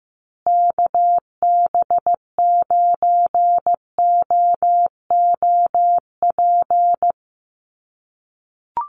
Morse code, heard as a keyed tone, about 15 wpm, 700 Hz. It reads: KB9OOP
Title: USA Call Signs Repeated 3x 15wpm